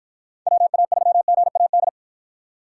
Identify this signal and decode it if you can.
Morse code, heard as a keyed tone, about 52 wpm, 700 Hz. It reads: WA3GAD